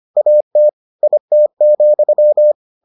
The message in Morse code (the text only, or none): AT IT,